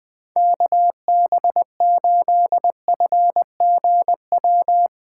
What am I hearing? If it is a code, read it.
Morse code, heard as a keyed tone, about 20 wpm, 700 Hz. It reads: KB8FGW